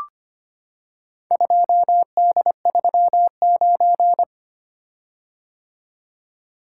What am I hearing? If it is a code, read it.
Morse code, heard as a keyed tone, about 25 wpm, 700 Hz. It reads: 2D39
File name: practice / 25wpm / Four Letter-Number Combinations - 25wpm